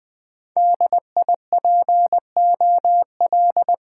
Morse code, heard as a keyed tone, about 20 wpm, 700 Hz. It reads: DIPOL
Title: Top 90 QSO Elements - 6x Character Spacing - 20wpm